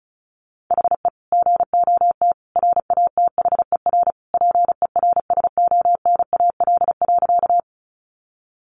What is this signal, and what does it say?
Morse code, heard as a keyed tone, about 34 wpm, 700 Hz. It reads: HE GOT RATHER PERSONAL.
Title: Sentences from Top 700 Words 35wpm